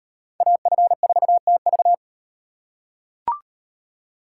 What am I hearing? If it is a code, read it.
Morse code, heard as a keyed tone, about 33 wpm, 700 Hz. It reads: AF4TV